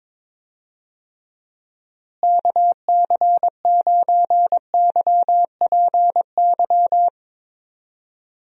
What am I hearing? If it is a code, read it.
Morse code, heard as a keyed tone, about 22 wpm, 700 Hz. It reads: KC9YPY
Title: USA Call Signs - 6 Characters - Repeated 2x - 22wpm